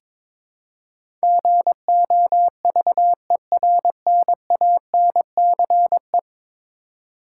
Morse code, heard as a keyed tone, about 22 wpm, 700 Hz. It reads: GOVERNANCE
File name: practice / 22wpm / 10 Letter Words - First 3 Letters Given - 22wpm